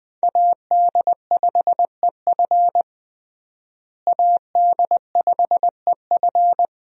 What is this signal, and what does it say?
Morse code, heard as a keyed tone, about 20 wpm, 700 Hz. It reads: AD5EF AD5EF
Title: USA Call Signs - 5 Characters - Repeated 2x - 20wpm